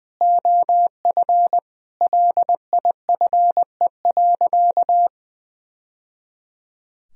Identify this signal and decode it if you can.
Morse code, heard as a keyed tone, about 20 wpm, 700 Hz. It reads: OF LIFE.
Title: Sentences from Top 1000 Words - Review Part 3 - 20wpm